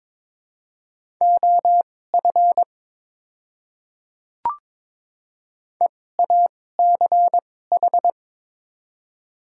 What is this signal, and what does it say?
Morse code, heard as a keyed tone, about 22 wpm, spaced as for 16 wpm, 700 Hz. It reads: OF EACH
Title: 2x Character Spacing - Top 100 Words - 22wpm